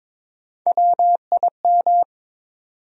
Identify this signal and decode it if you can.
Morse code, heard as a keyed tone, about 22 wpm, 700 Hz. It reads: WIM
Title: USA Call Sign Postfixes - 3 Letters 22wpm